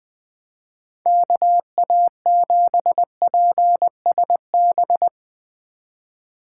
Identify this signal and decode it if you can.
Morse code, heard as a keyed tone, about 20 wpm, 700 Hz. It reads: KA7PSB
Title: Speed Racing - USA Call Signs 17wpm